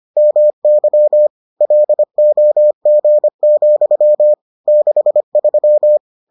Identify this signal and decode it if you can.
Morse code, heard as a keyed tone, about 25 wpm, 600 Hz. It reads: MY LOG, 63